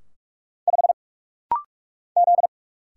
Morse code, heard as a keyed tone, about 44 wpm, 700 Hz. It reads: 5 Z
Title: Single Letter-Number - Rapid-Fire 45wpm